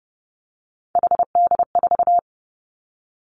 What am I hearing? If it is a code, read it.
Morse code, heard as a keyed tone, about 30 wpm, 700 Hz. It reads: HD4
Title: Three Challenging Letter-Number Combinations 30wpm